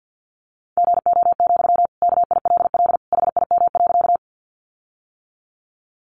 Morse code, heard as a keyed tone, about 50 wpm, 700 Hz. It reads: GO, KILL HIM.
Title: Sentences from Top 1400 Words 50wpm